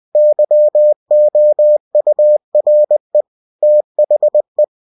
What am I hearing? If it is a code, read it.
Morse code, heard as a keyed tone, about 20 wpm, 600 Hz. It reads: YOURE THE
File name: cw-books / winnie-the-pooh / winnie-20wpm-page-0025